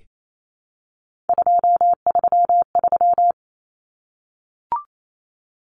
Morse code, heard as a keyed tone, about 28 wpm, 700 Hz. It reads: 233
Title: Common Cut Numbers - 3-Digit 28wpm